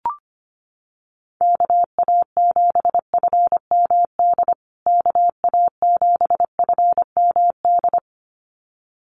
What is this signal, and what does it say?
Morse code, heard as a keyed tone, about 25 wpm, 700 Hz. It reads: KA7FMD KA7FMD
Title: USA Call Signs Repeated 2x 25wpm